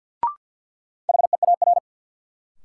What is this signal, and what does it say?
Morse code, heard as a keyed tone, about 53 wpm, 700 Hz. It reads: HEAR
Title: Sentences from Top 300 Words - New Word Drill - 50wpm